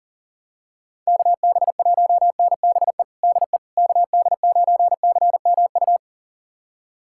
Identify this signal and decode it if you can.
Morse code, heard as a keyed tone, about 40 wpm, 700 Hz. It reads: KB1NBE DE KD9CMU
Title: USA Callsign DE Callsign - 6 Characters - 40wpm